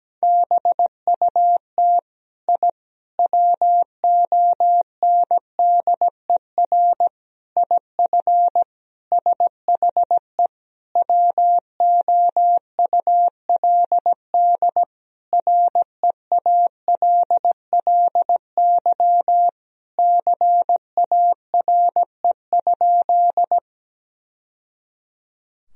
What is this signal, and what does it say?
Morse code, heard as a keyed tone, about 17 wpm, 700 Hz. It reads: BUT I WONDER IF SHE WOULD REALLY CARE?
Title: Sentences from Top 1000 Words - Review Part 3 - 17wpm